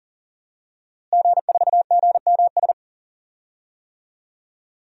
Morse code, heard as a keyed tone, about 40 wpm, 700 Hz. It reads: G4GMS
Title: DXCC-100 Call Signs - 5 Characters - Repeated 3x - 40wpm